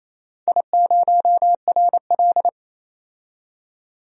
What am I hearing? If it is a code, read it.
Morse code, heard as a keyed tone, about 28 wpm, 700 Hz. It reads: I0RL
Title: DXCC-100 Call Signs - 4 Characters - 28wpm